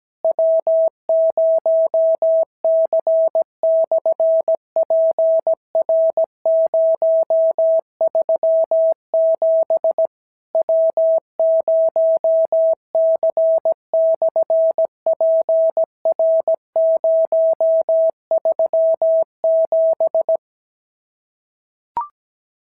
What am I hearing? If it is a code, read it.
Morse code, heard as a keyed tone, about 17 wpm, 650 Hz. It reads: W0C/PR037 W0C/PR037
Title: SOTA QSO Practice 17wpm